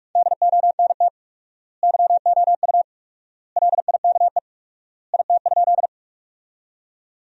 Morse code, heard as a keyed tone, about 42 wpm, 700 Hz. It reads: DONT YOU LIKE IT?
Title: 4x Word-Spacing - Sentences from Top 300 Words - 45wpm